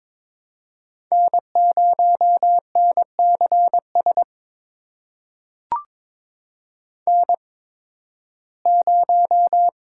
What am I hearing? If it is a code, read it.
Morse code, heard as a keyed tone, about 22 wpm, spaced as for 14 wpm, 700 Hz. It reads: N0NCS N0
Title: USA Call Signs - 5 Characters - 8x Character Spacing - 22wpm